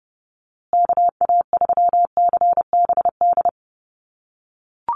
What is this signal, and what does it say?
Morse code, heard as a keyed tone, about 30 wpm, 700 Hz. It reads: KA3CBD
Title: USA Call Signs Repeated 3x 30wpm